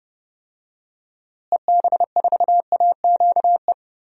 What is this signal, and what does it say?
Morse code, heard as a keyed tone, about 30 wpm, 700 Hz. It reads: EB4AQE
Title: DXCC-100 Call Signs - 6 Characters - Repeated 2x - 30wpm